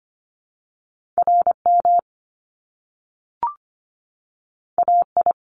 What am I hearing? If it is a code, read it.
Morse code, heard as a keyed tone, about 25 wpm, 700 Hz. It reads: RM AI
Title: Top Two Letter Combinations in Words 25wpm